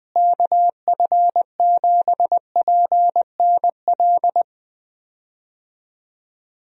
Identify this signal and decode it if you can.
Morse code, heard as a keyed tone, about 20 wpm, 700 Hz. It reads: KF7PNL
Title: USA Call Signs - 6 Characters - Repeated 2x - 20wpm